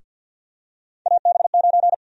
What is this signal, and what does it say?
Morse code, heard as a keyed tone, about 48 wpm, 700 Hz. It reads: AB9
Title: USA Call Sign Prefixes - Repeated 2x - 50wpm